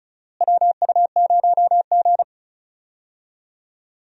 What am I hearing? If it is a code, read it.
Morse code, heard as a keyed tone, about 36 wpm, 700 Hz. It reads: WU0G
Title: USA Call Signs - 4 Characters - Repeated 2x - 35wpm